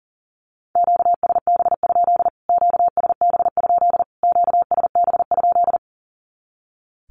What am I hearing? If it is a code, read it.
Morse code, heard as a keyed tone, about 40 wpm, 700 Hz. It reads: QSB? QSB? QSB?
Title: Q-Codes Repeated 3x 40wpm